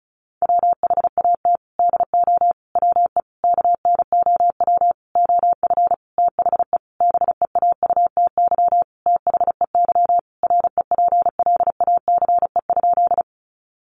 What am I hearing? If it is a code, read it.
Morse code, heard as a keyed tone, about 34 wpm, 700 Hz. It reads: WHAT DO WE KNOW OF THE BEAUTY THEY REPLACE?
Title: Sentences from Top 1400 Words 35wpm